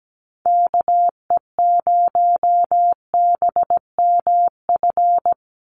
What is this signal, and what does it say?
Morse code, heard as a keyed tone, about 17 wpm, 700 Hz. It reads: KE0BMF
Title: USA Call Signs 17wpm